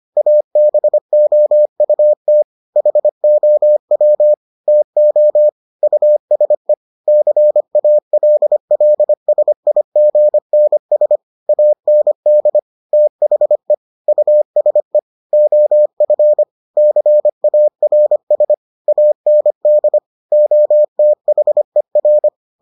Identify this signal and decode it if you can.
Morse code, heard as a keyed tone, about 25 wpm, 600 Hz. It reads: ABOUT HOW TO USE CALLSIGNS AND THE USE OF CARS AND OTHER